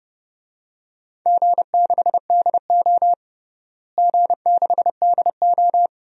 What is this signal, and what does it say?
Morse code, heard as a keyed tone, about 30 wpm, 700 Hz. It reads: G6DO G6DO
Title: DXCC-100 Call Signs - 4 Characters - Repeated 2x - 30wpm